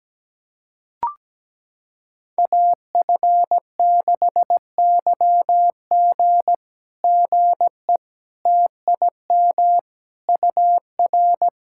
Morse code, heard as a keyed tone, about 17 wpm, 700 Hz. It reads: AF6YG GE TIM UR
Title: SOTA QSO Practice 17wpm